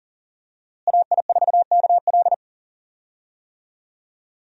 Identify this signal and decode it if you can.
Morse code, heard as a keyed tone, about 37 wpm, 700 Hz. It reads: AI4KL
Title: USA Call Signs - 5 Characters - Repeated 2x - 40wpm